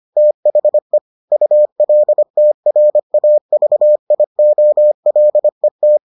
Morse code, heard as a keyed tone, about 25 wpm, 600 Hz. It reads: THE ULTRAVIOLET